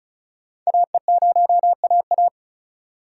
Morse code, heard as a keyed tone, about 34 wpm, 700 Hz. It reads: AE0AA
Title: USA Call Signs - 5 Characters - 4x Character Spacing - 35wpm